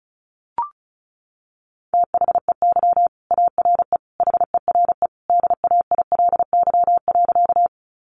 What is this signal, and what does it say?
Morse code, heard as a keyed tone, about 34 wpm, 700 Hz. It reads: THEY ARE HERE DAILY.
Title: Sentences from Top 1100 Words 35wpm